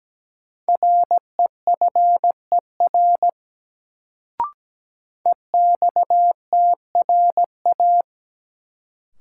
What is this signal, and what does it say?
Morse code, heard as a keyed tone, about 17 wpm, 700 Hz. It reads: REFER EXTRA
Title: Sentences from Top 1000 Words - New Word Drill - 17wpm